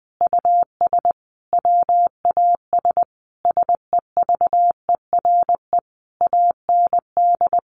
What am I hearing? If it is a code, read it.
Morse code, heard as a keyed tone, about 20 wpm, 700 Hz. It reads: US WAS SEVERE AND
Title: Sentences from Top 1400 Words 20wpm